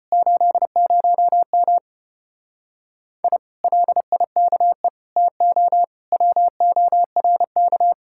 Morse code, heard as a keyed tone, about 33 wpm, 700 Hz. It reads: 80M I LIKE TO WORK